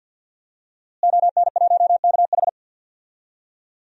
Morse code, heard as a keyed tone, about 52 wpm, 700 Hz. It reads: ON1KH
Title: DXCC-100 Call Signs - 5 Characters - Repeated 3x - 50wpm